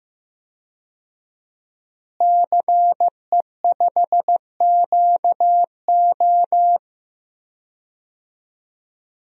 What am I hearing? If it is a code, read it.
Morse code, heard as a keyed tone, about 16 wpm, 700 Hz. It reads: CE5QO
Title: DXCC-100 Call Signs - 5 Characters - Repeated 2x - 15wpm